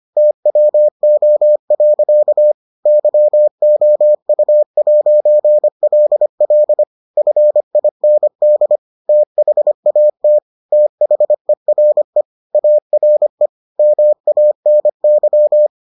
Morse code, heard as a keyed tone, about 25 wpm, 600 Hz. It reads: TWO. YOU'LL FIND THAT THERE ARE MANY